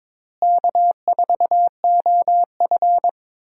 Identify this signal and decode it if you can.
Morse code, heard as a keyed tone, about 22 wpm, 700 Hz. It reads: K4OF
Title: USA Call Signs - 4 Characters - Repeated 2x - 22wpm